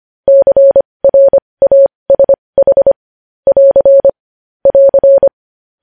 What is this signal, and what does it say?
Morse code, heard as a keyed tone, about 25 wpm, spaced as for 20 wpm, 550 Hz. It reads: CRASH <AR> <AR>
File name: podcast / News headlines-1x-25x20